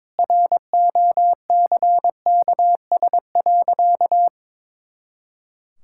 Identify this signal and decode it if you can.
Morse code, heard as a keyed tone, about 22 wpm, 700 Hz. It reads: ROCKS.
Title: Pangram Challenge v2 - 22wpm